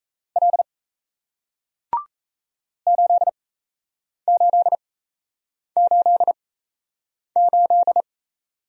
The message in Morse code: L 8888

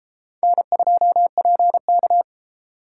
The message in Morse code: N2PK

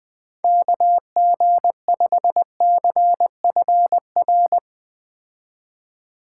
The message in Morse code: KG5CFR

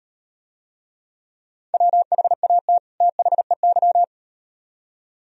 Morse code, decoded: WHAT THEY

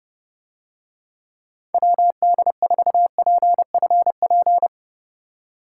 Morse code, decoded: WD4PFP

